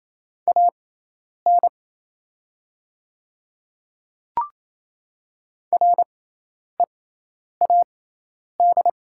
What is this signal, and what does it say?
Morse code, heard as a keyed tone, about 28 wpm, spaced as for 10 wpm, 700 Hz. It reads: AN READ
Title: 6x Character Spacing - Top 500 Words - 28wpm